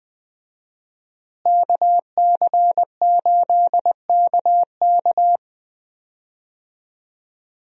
Morse code, heard as a keyed tone, about 20 wpm, 700 Hz. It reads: KC8KK
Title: USA Call Signs - 5 Characters - Repeated 2x - 20wpm